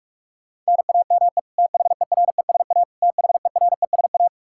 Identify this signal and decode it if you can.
Morse code, heard as a keyed tone, about 46 wpm, 700 Hz. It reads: NAME THERESA THERESA
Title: Practice QSO - Exchange 1 - Beginning - 45wpm